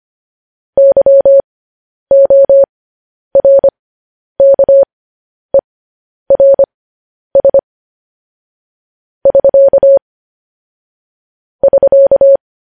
Morse code, heard as a keyed tone, about 25 wpm, spaced as for 10 wpm, 550 Hz. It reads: YORKERS <SK> <SK>
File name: podcast / News headlines-1x-25x10